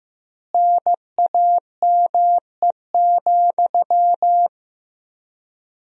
Morse code, heard as a keyed tone, about 15 wpm, 700 Hz. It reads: NAME,
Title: 3x Word-Spacing - Sentences from Top 500 Words - 15wpm